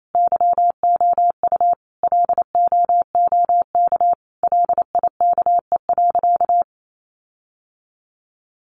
Morse code, heard as a keyed tone, about 28 wpm, 700 Hz. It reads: YOU LOOK LIKE.
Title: Sentences from Top 1300 Words 28wpm